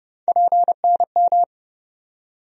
Morse code, heard as a keyed tone, about 30 wpm, 700 Hz. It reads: PNM
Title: USA Call Sign Postfixes - 3 Letters 30wpm